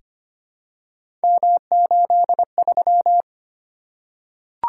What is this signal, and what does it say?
Morse code, heard as a keyed tone, about 25 wpm, 700 Hz. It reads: M83
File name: Three Letter-Number Combinations - 25wpm